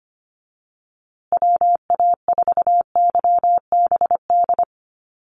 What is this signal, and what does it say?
Morse code, heard as a keyed tone, about 25 wpm, 700 Hz. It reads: WA4YBD
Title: USA Call Signs Repeated 3x 25wpm